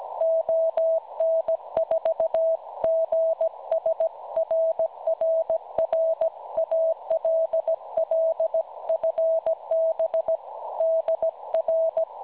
Morse code, heard as a keyed tone, about 17 wpm, 650 Hz. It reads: ON4GS RRR ALL FB DR
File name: cwronde / websdr recording .6kHz